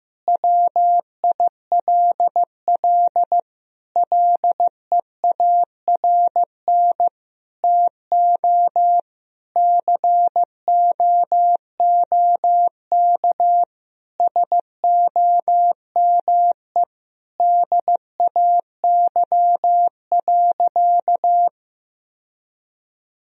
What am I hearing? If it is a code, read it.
Morse code, heard as a keyed tone, about 15 wpm, 700 Hz. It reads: WILL LEARN TO COOK SOME DAY.
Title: Sentences from Top 1000 Words - Review Part 4 - 15wpm